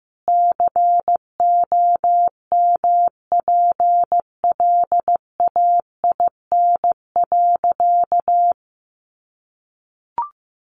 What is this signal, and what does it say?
Morse code, heard as a keyed tone, about 15 wpm, 700 Hz. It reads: COMPLAIN.
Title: Sentences from Top 1300 Words 15wpm